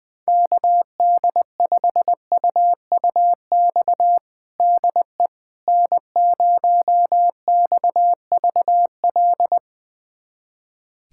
Morse code, heard as a keyed tone, about 21 wpm, 700 Hz. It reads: KD5UUX DE N0XVL
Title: USA Callsign DE Callsign 20wpm